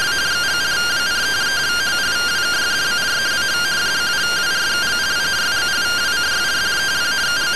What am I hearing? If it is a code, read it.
Two-tone teleprinter signal, about 170 Hz shift.